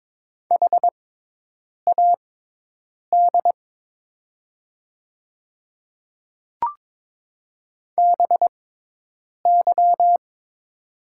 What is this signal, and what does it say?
Morse code, heard as a keyed tone, about 22 wpm, spaced as for 8 wpm, 700 Hz. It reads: HAD BY
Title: 6x Character Spacing - Top 200 Words - 22wpm